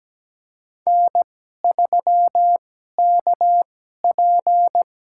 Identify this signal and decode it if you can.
Morse code, heard as a keyed tone, about 17 wpm, spaced as for 12 wpm, 700 Hz. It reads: N3KP
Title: USA Call Signs - 2x Character Spacing - 17wpm